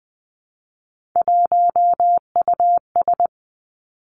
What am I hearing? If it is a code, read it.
Morse code, heard as a keyed tone, about 20 wpm, 700 Hz. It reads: 1US
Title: Three Challenging Letter-Number Combinations 20wpm